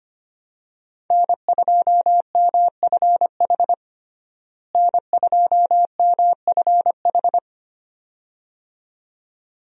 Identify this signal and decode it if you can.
Morse code, heard as a keyed tone, about 25 wpm, 700 Hz. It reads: N2MFH N2MFH
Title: USA Call Signs - 5 Characters - Repeated 2x - 25wpm